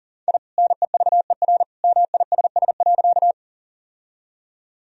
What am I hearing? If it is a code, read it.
Morse code, heard as a keyed tone, about 40 wpm, 700 Hz. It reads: I NEVER MISS.
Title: Sentences from Top 1000 Words - Review Part 3 - 40wpm